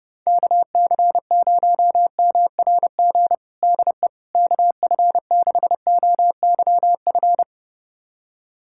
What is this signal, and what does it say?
Morse code, heard as a keyed tone, about 30 wpm, 700 Hz. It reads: KC0MRG DE KF6OYF
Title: USA Callsign DE Callsign - 6 Characters - 30wpm